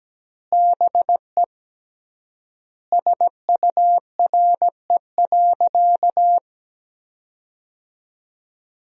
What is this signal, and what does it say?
Morse code, heard as a keyed tone, about 17 wpm, 700 Hz. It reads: BE SURE.
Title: 3x Word-Spacing - Sentences from Top 500 Words - 17wpm